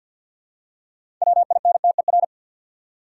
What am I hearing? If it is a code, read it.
Morse code, heard as a keyed tone, about 52 wpm, 700 Hz. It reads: WINTER
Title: Six Letter Words - 50wpm